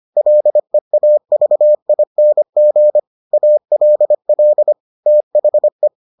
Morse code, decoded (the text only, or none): LEAVING ALL THE